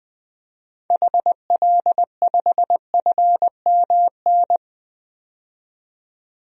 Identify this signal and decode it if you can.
Morse code, heard as a keyed tone, about 21 wpm, 700 Hz. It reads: HL5FMN
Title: DXCC-100 Call Signs - 6 Characters - 20wpm